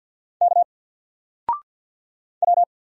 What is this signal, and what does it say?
Morse code, heard as a keyed tone, about 48 wpm, 700 Hz. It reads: K W
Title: Single Letter-Number Alt 50wpm